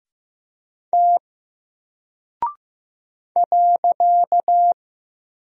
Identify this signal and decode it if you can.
Morse code, heard as a keyed tone, about 15 wpm, 700 Hz. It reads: T .
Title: Single Character (Letter:Number:Punctuation:Prosign) Alt 15wpm